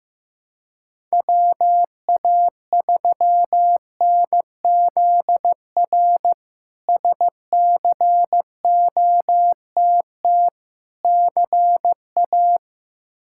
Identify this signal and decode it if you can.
Morse code, heard as a keyed tone, about 16 wpm, 700 Hz. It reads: WA3NZR SCOTT CA
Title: K1USN SST QSO Practice - 15wpm